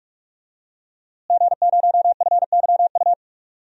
Morse code, heard as a keyed tone, about 46 wpm, 700 Hz. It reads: G0FYU
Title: DXCC-100 Call Signs - 5 Characters - Repeated 2x - 45wpm